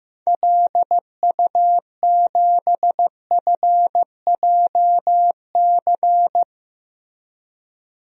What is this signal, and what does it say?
Morse code, heard as a keyed tone, about 15 wpm, 700 Hz. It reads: LU7FJC
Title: DXCC-100 Call Signs - 6 Characters - Repeated 3x - 15wpm